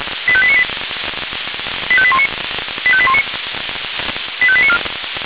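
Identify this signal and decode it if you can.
Five-tone selective call (ZVEI), heard as quick runs of five tones, tones of about 70 ms. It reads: 85898 85819 85819 85894
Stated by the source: ZVEI five-tone call received by SDR.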